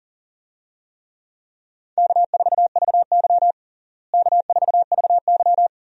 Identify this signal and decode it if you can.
Morse code, heard as a keyed tone, about 40 wpm, 700 Hz. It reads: K4VY K4VY
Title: USA Call Signs - 4 Characters - Repeated 2x - 40wpm